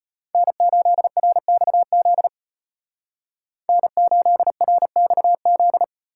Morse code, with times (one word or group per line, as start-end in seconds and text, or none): 0.3-2.3 N8RXZ
3.7-5.8 N8RXZ